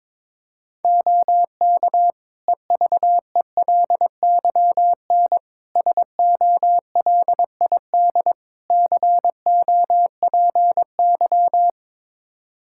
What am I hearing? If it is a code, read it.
Morse code, heard as a keyed tone, about 23 wpm, 700 Hz. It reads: OK EVELYN SOLID COPY
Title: Practice QSO - Exchange 3 - More Info - 22wpm